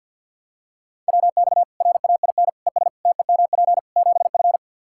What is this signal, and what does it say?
Morse code, heard as a keyed tone, about 47 wpm, 700 Hz. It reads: WX RAIN ES TEMP 7F
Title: Practice QSO - Exchange 2 - Info - 50wpm